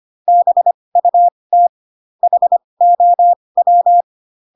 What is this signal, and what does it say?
Morse code, heard as a keyed tone, about 25 wpm, spaced as for 20 wpm, 700 Hz. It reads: BUT HOW
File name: Pinocchio / Pinocchio ch8 15 25 ik1hge